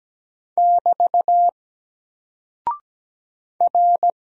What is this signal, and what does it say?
Morse code, heard as a keyed tone, about 17 wpm, 700 Hz. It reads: <BT> R
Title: Single Character (Letter:Number:Punctuation:Prosign) Alt 17wpm